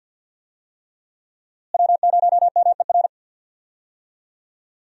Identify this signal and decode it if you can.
Morse code, heard as a keyed tone, about 51 wpm, 700 Hz. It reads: W0MER
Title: USA Call Signs - 5 Characters - Repeated 3x - 50wpm